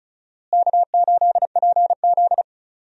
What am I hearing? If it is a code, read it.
Morse code, heard as a keyed tone, about 36 wpm, 700 Hz. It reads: K8PZ
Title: USA Call Signs - 4 Characters - Repeated 3x - 35wpm